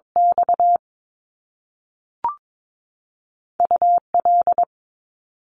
Morse code, heard as a keyed tone, about 22 wpm, spaced as for 3 wpm, 700 Hz. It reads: XU L
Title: Top Two Letter Combinations in Words 22wpm